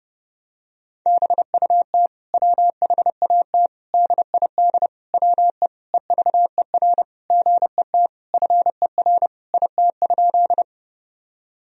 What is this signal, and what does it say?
Morse code, heard as a keyed tone, about 30 wpm, 700 Hz. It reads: BUT WHAT DID WE EVER GET FER IT?
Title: Sentences from Top 400 Words and Abbreviations 30wpm